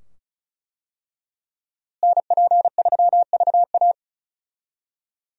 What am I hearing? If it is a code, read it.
Morse code, heard as a keyed tone, about 36 wpm, 700 Hz. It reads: NP3VA
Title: USA Call Signs - 5 Characters - Repeated 2x - 35wpm